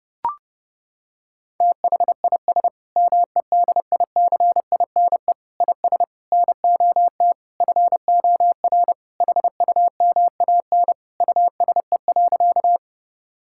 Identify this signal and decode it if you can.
Morse code, heard as a keyed tone, about 28 wpm, 700 Hz. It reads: THIS MEDICINE IS NOT FOR HUMAN USE.
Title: Sentences from Top 1000 Words - Review Part 6 - 30wpm